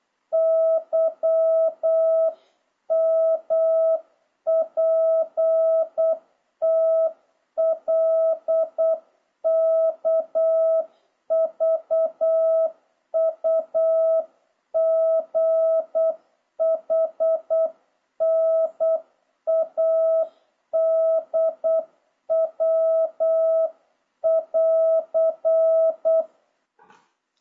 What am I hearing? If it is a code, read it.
Morse code, heard as a keyed tone, about 8 wpm, 650 Hz. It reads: YMPTLKVUGHNADW<AR>
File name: jh8cbh / 2026cw